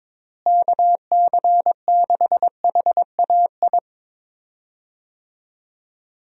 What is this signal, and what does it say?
Morse code, heard as a keyed tone, about 22 wpm, 700 Hz. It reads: KC6HAI
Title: USA Call Signs - 6 Characters - Repeated 3x - 22wpm